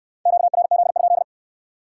Morse code, heard as a keyed tone, about 67 wpm, 700 Hz. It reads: QRZ?